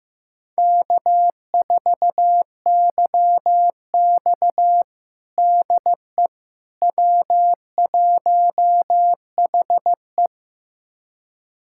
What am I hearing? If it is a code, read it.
Morse code, heard as a keyed tone, about 15 wpm, 700 Hz. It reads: K4YX DE W1HE
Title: USA Callsign DE Callsign - 4 Characters - 15wpm